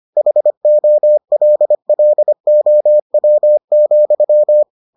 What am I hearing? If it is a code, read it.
Morse code, heard as a keyed tone, about 25 wpm, 600 Hz. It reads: HOLLOW,